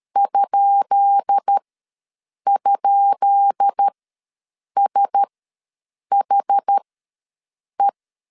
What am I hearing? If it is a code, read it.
Morse code, heard as a keyed tone, about 13 wpm, spaced as for 7 wpm, 800 Hz. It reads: ??SHE
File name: LEZIONI / 10 SIGNS / 10B INT ws 12wpm 3527